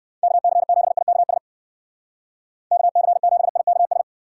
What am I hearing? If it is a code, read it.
Morse code, heard as a keyed tone, about 57 wpm, 700 Hz. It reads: KC7IKS KC7IKS